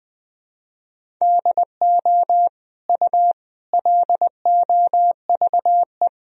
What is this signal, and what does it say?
Morse code, heard as a keyed tone, about 20 wpm, 700 Hz. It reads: DO U LOVE
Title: Sentences from Top 400 Words and Abbreviations 20wpm